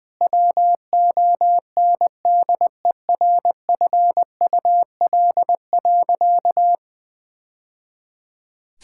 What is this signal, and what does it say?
Morse code, heard as a keyed tone, about 21 wpm, 700 Hz. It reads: WONDERFUL.
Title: Sentences with 3-Word Phrases - Spoken Beginning - 20wpm